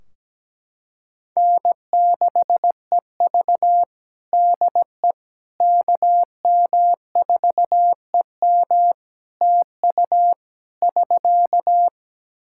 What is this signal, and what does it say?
Morse code, heard as a keyed tone, about 17 wpm, 700 Hz. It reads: N6EV DE KM4EM TU <SK>
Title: Practice QSO - Exchange 4 - Ending - 17wpm